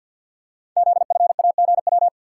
Morse code, heard as a keyed tone, about 48 wpm, 700 Hz. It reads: ZFAGW
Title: Code Groups - 5 Letters - 50wpm